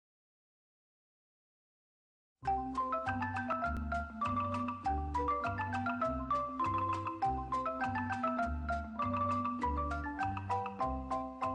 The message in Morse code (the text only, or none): none